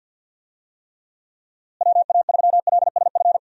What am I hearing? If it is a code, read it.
Morse code, heard as a keyed tone, about 52 wpm, 700 Hz. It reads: WA3LSF
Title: USA Call Signs - 6 Characters - Repeated 2x - 50wpm